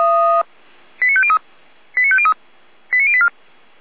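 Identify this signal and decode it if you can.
Five-tone selective call (ZVEI), heard as quick runs of five tones, tones of about 70 ms. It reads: 89503 89503 89084
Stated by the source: ZVEI five-tone selective call received on SDR.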